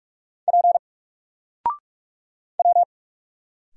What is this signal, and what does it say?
Morse code, heard as a keyed tone, about 44 wpm, 700 Hz. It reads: P W
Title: Single Character (Letter:Number:Punctuation:Prosign) 45wpm